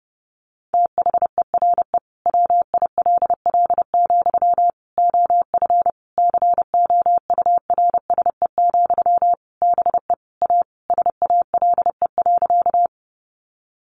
Morse code, heard as a keyed tone, about 30 wpm, 700 Hz. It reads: THERE WILL, OF COURSE, BE A SALE.
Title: Sentences from Top 1400 Words 30wpm